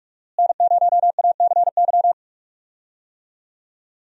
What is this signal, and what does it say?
Morse code, heard as a keyed tone, about 44 wpm, 700 Hz. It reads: N0ACY
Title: USA Call Signs - 5 Characters - Repeated 2x - 45wpm